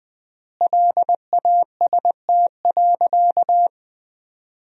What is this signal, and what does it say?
Morse code, heard as a keyed tone, about 20 wpm, 700 Hz. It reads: LAST.
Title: 3x Word-Spacing - Sentences from Top 300 Words - 20wpm